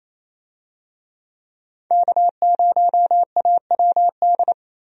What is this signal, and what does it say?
Morse code, heard as a keyed tone, about 28 wpm, 700 Hz. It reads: K0AWD
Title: USA Call Signs - 5 Characters - Repeated 2x - 28wpm